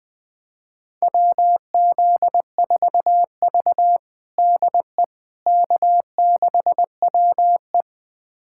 Morse code, heard as a keyed tone, about 21 wpm, 700 Hz. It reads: WZ4V DE K6WE
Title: USA Callsign DE Callsign - 4 Characters - 20wpm